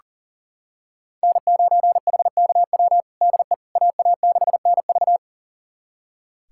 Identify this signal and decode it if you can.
Morse code, heard as a keyed tone, about 40 wpm, 700 Hz. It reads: N9HKW DE AA6NV
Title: USA Callsign DE Callsign - 5 Characters - 40wpm